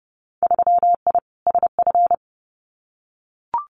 Morse code, heard as a keyed tone, about 30 wpm, 700 Hz. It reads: 3I SF
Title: Winter Field Day Exchanges 30wpm